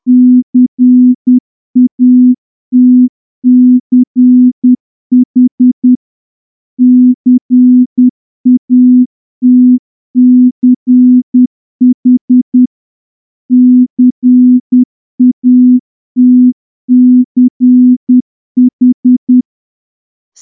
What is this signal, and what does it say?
Morse code, heard as a keyed tone, about 10 wpm, 250 Hz. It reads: CATCH CATCH CATCH
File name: morse-training-10WPM